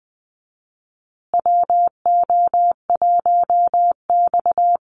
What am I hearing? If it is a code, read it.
Morse code, heard as a keyed tone, about 20 wpm, 700 Hz. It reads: WO1X
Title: Call Signs Repeated 2x - 20wpm